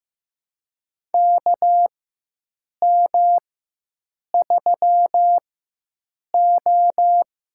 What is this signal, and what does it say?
Morse code, heard as a keyed tone, about 16 wpm, spaced as for 7 wpm, 700 Hz. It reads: KM3O
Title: USA Call Signs - 4 Characters - 4x Character Spacing - 15wpm